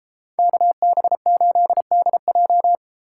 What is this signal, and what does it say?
Morse code, heard as a keyed tone, about 31 wpm, 700 Hz. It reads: KB8DJ